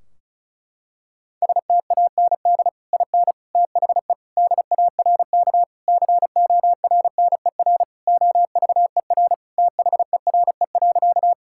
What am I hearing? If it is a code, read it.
Morse code, heard as a keyed tone, about 36 wpm, 700 Hz. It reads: STAND IN THE DARK CORNER OVER THERE.
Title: Sentences from Top 1600 Words 35wpm